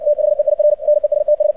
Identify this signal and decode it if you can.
Morse code, heard as a keyed tone, about 60 wpm, 600 Hz, with radio noise. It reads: THEIR NESTS
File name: archived / CRSnet